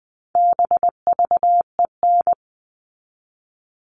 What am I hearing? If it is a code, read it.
Morse code, heard as a keyed tone, about 20 wpm, 700 Hz. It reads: BVEN
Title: All Cut Numbers - 4-Digit 20wpm